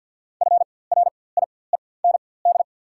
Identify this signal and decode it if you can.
Morse code, heard as a keyed tone, about 54 wpm, spaced as for 24 wpm, 700 Hz. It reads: FRIEND
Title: 4x Character Spacing - Top 300 Words - 50wpm